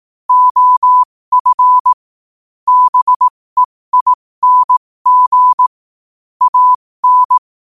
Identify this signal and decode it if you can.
Morse code, heard as a keyed tone, about 18 wpm, spaced as for 15 wpm, 1000 Hz. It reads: OF BEING AN
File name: qotdfiles / 15wpm